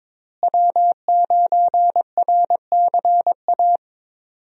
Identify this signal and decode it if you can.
Morse code, heard as a keyed tone, about 22 wpm, 700 Hz. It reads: W9RCA
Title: USA Call Signs - 5 Characters - Repeated 2x - 22wpm